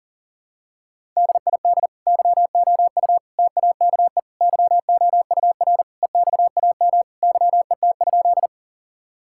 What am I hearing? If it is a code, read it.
Morse code, heard as a keyed tone, about 40 wpm, 700 Hz. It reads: DID YOU TAKE YOUR EXAM YET?
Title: Sentences from Top 1000 Words - Review Part 5 - 40wpm